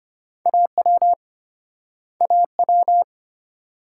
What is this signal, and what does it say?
Morse code, heard as a keyed tone, about 35 wpm, 700 Hz. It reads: AW AW